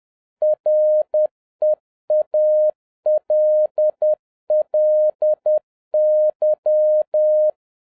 Morse code, heard as a keyed tone, about 10 wpm, 610 Hz. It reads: REALLY
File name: 2011-07-mp3 / n56 BACK FROM A BIG ACCIDENT 10wpm